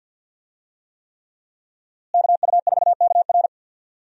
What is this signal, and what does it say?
Morse code, heard as a keyed tone, about 52 wpm, 700 Hz. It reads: KU4KR
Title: USA Call Signs - 5 Characters - Repeated 3x - 50wpm